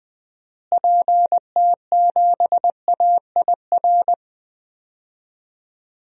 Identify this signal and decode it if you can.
Morse code, heard as a keyed tone, about 21 wpm, 700 Hz. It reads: PT7AIR
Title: DXCC-100 Call Signs - 6 Characters - Repeated 2x - 20wpm